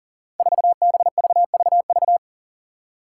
Morse code, heard as a keyed tone, about 40 wpm, 700 Hz. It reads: 4BVVV
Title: Five Challenging Letter-Number Combinations 40wpm